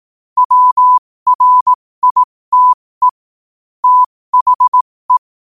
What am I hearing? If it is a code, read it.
Morse code, heard as a keyed tone, about 18 wpm, spaced as for 15 wpm, 1000 Hz. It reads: WRITE THE